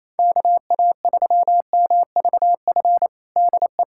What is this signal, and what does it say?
Morse code, heard as a keyed tone, about 27 wpm, 700 Hz. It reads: KA3MVF DE
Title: USA Callsign DE Callsign 28wpm